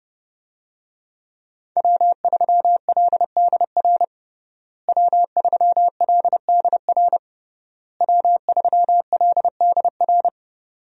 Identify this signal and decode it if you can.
Morse code, heard as a keyed tone, about 32 wpm, 700 Hz. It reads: W3LDR W3LDR W3LDR
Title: USA Call Signs - 5 Characters - Repeated 3x - 30wpm